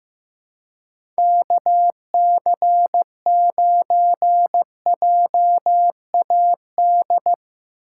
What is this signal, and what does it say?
Morse code, heard as a keyed tone, about 15 wpm, 700 Hz. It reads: KC9JAD
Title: USA Call Signs - 4x Character Spacing - 15wpm